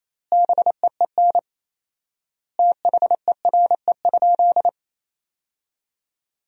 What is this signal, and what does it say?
Morse code, heard as a keyed tone, about 28 wpm, 700 Hz. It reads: BEEN THERE?
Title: 4x Word-Spacing - Sentences from Top 300 Words - 28wpm